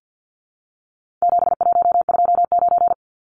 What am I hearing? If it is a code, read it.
Morse code, heard as a keyed tone, about 50 wpm, 700 Hz. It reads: 7129
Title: Numbers - 4-Digit 50wpm